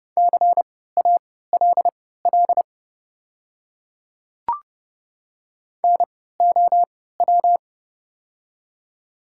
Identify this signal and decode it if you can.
Morse code, heard as a keyed tone, about 28 wpm, spaced as for 17 wpm, 690 Hz. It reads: CALL NOW
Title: 3x Character Spacing - Top 100 Words - 30wpm